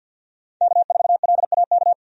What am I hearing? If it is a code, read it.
Morse code, heard as a keyed tone, about 52 wpm, 700 Hz. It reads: K4LAK
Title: USA Call Signs - 5 Characters - 50wpm